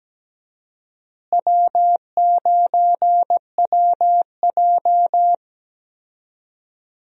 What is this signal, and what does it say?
Morse code, heard as a keyed tone, about 17 wpm, 700 Hz. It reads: W9WJ
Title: USA Call Signs - 4 Characters - Repeated 3x - 17wpm